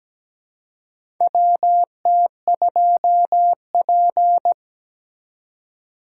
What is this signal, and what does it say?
Morse code, heard as a keyed tone, about 17 wpm, 700 Hz. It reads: WT2P
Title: USA Call Signs - 4 Characters - 3x Character Spacing - 17wpm